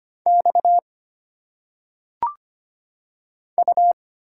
Morse code, heard as a keyed tone, about 25 wpm, 700 Hz. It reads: X U
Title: Single Letters - Phonetic Pronunciation 25wpm